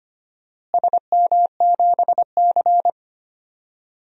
Morse code, heard as a keyed tone, about 25 wpm, 700 Hz. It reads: SM7C
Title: DXCC-100 Call Signs - 4 Characters - Repeated 3x - 25wpm